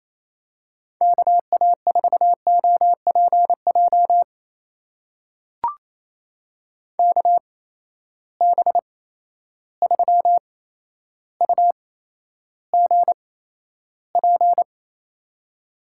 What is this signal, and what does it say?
Morse code, heard as a keyed tone, about 28 wpm, spaced as for 12 wpm, 700 Hz. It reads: KA4OPJ KB3UGP
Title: USA Call Signs - 8x Character Spacing - 28wpm